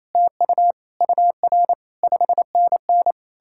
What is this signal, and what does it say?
Morse code, heard as a keyed tone, about 37 wpm, 700 Hz. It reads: TU UR 5NN